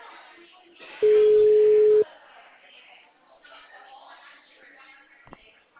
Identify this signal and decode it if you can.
Ringback tone (Europe, 425 Hz), with speech.
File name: out- -2007- - - .414